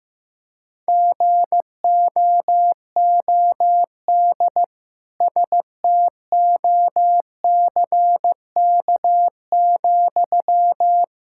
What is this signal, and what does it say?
Morse code, heard as a keyed tone, about 15 wpm, 700 Hz. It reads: GOOD STOCK,
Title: Sentences from Top 1000 Words - Review Part 5 - 15wpm